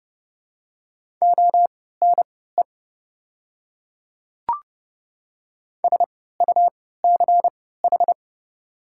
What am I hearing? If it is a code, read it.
Morse code, heard as a keyed tone, about 30 wpm, spaced as for 17 wpm, 700 Hz. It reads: ONE SUCH
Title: 3x Character Spacing - Top 200 Words - 30wpm